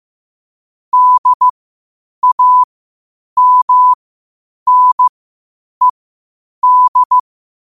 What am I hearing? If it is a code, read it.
Morse code, heard as a keyed tone, about 15 wpm, spaced as for 8 wpm, 1000 Hz. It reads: DAMNED
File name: qotdfiles / 8wpm